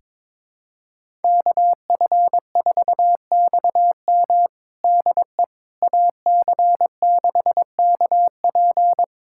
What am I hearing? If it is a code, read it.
Morse code, heard as a keyed tone, about 22 wpm, 700 Hz. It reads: KF4XM DE AC6KP
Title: USA Callsign DE Callsign - 5 Characters - 22wpm